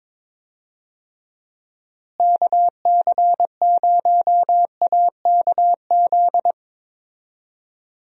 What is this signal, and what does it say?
Morse code, heard as a keyed tone, about 22 wpm, 700 Hz. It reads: KC0AKZ
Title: USA Call Signs - 6 Characters - Repeated 2x - 22wpm